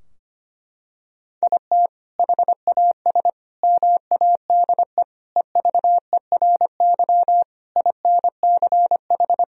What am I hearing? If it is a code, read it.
Morse code, heard as a keyed tone, about 25 wpm, 700 Hz. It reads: IT HAS MADE EVERY INCH
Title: Sentences from Top 1000 Words - Review Part 3 - 25wpm